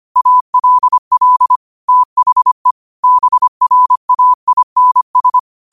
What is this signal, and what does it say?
Morse code, heard as a keyed tone, about 25 wpm, 1000 Hz. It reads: ALL THE BRAINS
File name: qotdfiles / 25wpm